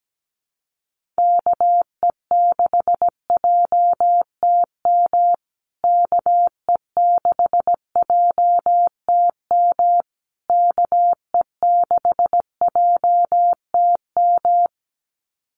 Morse code, heard as a keyed tone, about 17 wpm, 700 Hz. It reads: KE6JTM KE6JTM KE6JTM
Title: USA Call Signs Repeated 3x 17wpm